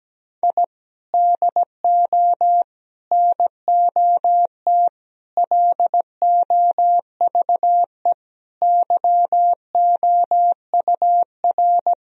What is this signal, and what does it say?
Morse code, heard as a keyed tone, about 17 wpm, 700 Hz. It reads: I DO NOT LOVE YOUR
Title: Sentences from Top 1000 Words - Review Part 4 - 17wpm